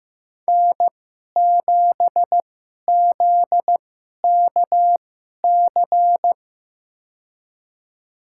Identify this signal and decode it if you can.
Morse code, heard as a keyed tone, about 15 wpm, spaced as for 11 wpm, 700 Hz. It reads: N7ZKC
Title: USA Call Signs - 5 Characters - 2x Character Spacing - 15wpm